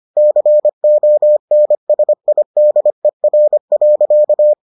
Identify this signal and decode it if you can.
Morse code, heard as a keyed tone, about 25 wpm, 600 Hz. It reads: CONSIDER.